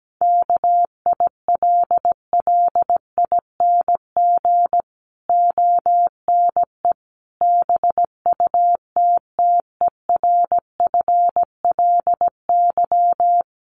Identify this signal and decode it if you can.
Morse code, heard as a keyed tone, about 17 wpm, 700 Hz. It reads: KILLING ONE BUTTERFLY
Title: A Sound Of Thunder 17wpm